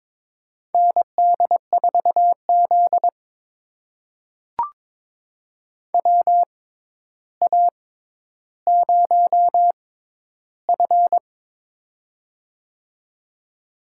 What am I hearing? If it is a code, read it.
Morse code, heard as a keyed tone, about 22 wpm, spaced as for 11 wpm, 700 Hz. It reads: ND4Z WA0F
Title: USA Call Signs - 4 Characters - 6x Character Spacing - 22wpm